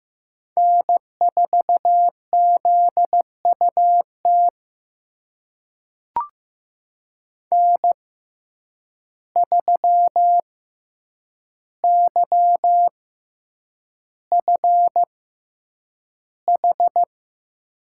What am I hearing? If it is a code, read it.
Morse code, heard as a keyed tone, about 15 wpm, spaced as for 8 wpm, 700 Hz. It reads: N4ZUT N3YFH
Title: USA Call Signs - 5 Characters - 6x Character Spacing - 15wpm